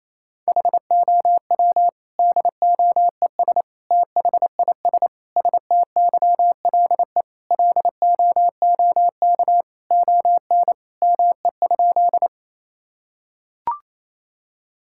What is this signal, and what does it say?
Morse code, heard as a keyed tone, about 28 wpm, 700 Hz. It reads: HOW DOES THIS STYLE LOOK ON ME?
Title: Sentences from Top 1000 Words - Review Part 5 - 28wpm